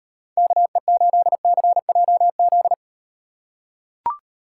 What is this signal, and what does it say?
Morse code, heard as a keyed tone, about 38 wpm, 700 Hz. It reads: KE8CJZ